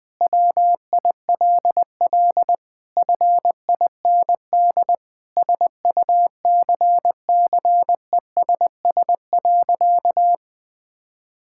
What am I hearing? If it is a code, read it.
Morse code, heard as a keyed tone, about 20 wpm, 700 Hz. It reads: WILL FIND SUCCESS.
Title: Sentences from Top 1000 Words - Review Part 5 - 20wpm